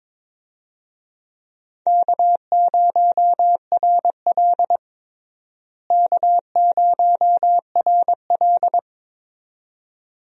Morse code, heard as a keyed tone, about 22 wpm, 700 Hz. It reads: K0RL K0RL
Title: USA Call Signs - 4 Characters - Repeated 2x - 22wpm